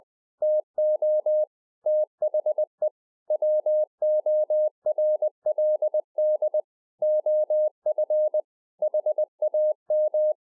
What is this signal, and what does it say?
Morse code, heard as a keyed tone, about 20 wpm, 610 Hz. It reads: TO THE WORLD OF HAM